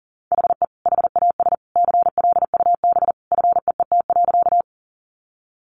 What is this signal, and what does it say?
Morse code, heard as a keyed tone, about 40 wpm, 700 Hz. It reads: HE HAS CLUB FEET.
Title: Sentences from Top 1200 Words 40wpm